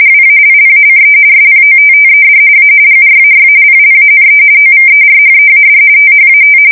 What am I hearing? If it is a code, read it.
RTTY, about 45.45 baud. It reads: WALLACE SPOTTED A GRAY PLANE OVERHEAD THAT
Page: 170 Shift 2125/2295